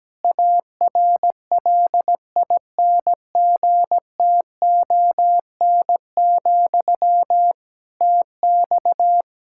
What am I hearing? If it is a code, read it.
Morse code, heard as a keyed tone, about 17 wpm, 700 Hz. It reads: ARLINGTON, TX
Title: Practice QSO - Exchange 1 - Beginning - 17wpm